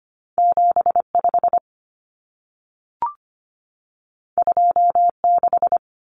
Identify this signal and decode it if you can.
Morse code, heard as a keyed tone, about 25 wpm, 700 Hz. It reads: 7 5 2 6
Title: Numbers - 2-Digit 25wpm